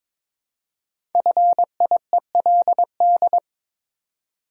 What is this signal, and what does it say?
Morse code, heard as a keyed tone, about 23 wpm, 700 Hz. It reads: FIELD
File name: 5 Letter Words - First 3 Letters Given - 22wpm